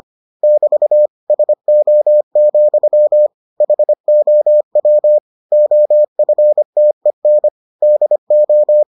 Morse code, heard as a keyed tone, about 25 wpm, 600 Hz. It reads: <BT> SO, HOW OFTEN DO